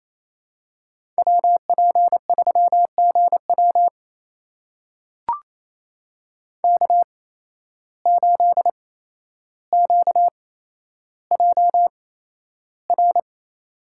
Morse code, heard as a keyed tone, about 28 wpm, spaced as for 12 wpm, 700 Hz. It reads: WP3GW K8QJR
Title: USA Call Signs - 5 Characters - 8x Character Spacing - 28wpm